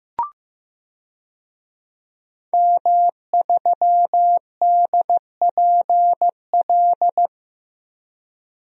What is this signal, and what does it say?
Morse code, heard as a keyed tone, about 15 wpm, 700 Hz. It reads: M3DPL
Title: DXCC-100 Call Signs - 5 Characters - Repeated 3x - 15wpm